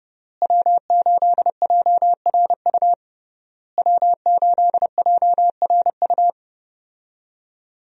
Morse code, heard as a keyed tone, about 30 wpm, 700 Hz. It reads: W8JRU W8JRU
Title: USA Call Signs - 5 Characters - Repeated 3x - 30wpm